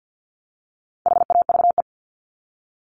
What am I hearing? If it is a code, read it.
Morse code, heard as a keyed tone, about 50 wpm, 700 Hz. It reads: HAVE
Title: Top 200 Words Repeated 50wpm